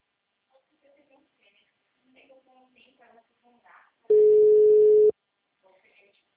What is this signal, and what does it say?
Ringback tone (Europe, 425 Hz), with speech.